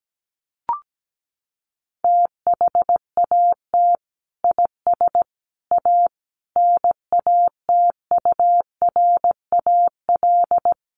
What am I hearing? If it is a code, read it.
Morse code, heard as a keyed tone, about 17 wpm, 700 Hz. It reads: THAT IS A NATURAL
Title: Sentences from Top 1200 Words 17wpm